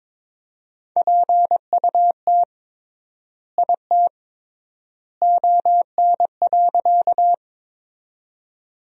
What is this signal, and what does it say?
Morse code, heard as a keyed tone, about 21 wpm, 700 Hz. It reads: PUT IT ON.
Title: 3x Word-Spacing - Sentences from Top 200 Words - 22wpm